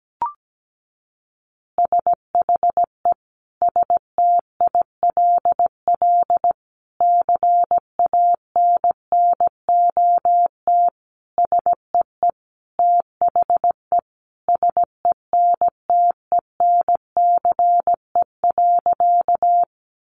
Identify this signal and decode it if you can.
Morse code, heard as a keyed tone, about 17 wpm, 700 Hz. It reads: SHE STILL CANNOT SEE THE SENTENCE.
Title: Additional Easy Phrases 17wpm